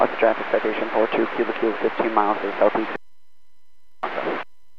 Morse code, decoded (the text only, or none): none